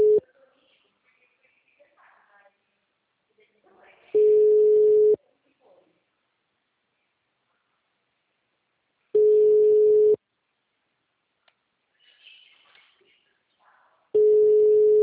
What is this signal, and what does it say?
Ringback tone (Europe, 425 Hz), with speech.
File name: out- -2007- - - .5